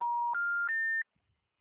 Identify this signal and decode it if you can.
Special information tone.